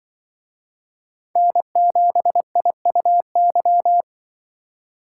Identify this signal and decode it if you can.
Morse code, heard as a keyed tone, about 24 wpm, 700 Hz. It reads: N7IUY
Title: Speed Racing - USA 1x3 Call Signs 20wpm